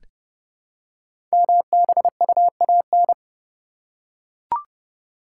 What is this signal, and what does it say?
Morse code, heard as a keyed tone, about 28 wpm, 700 Hz. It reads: MBUAN
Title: Code Groups - 5 Letters - 30wpm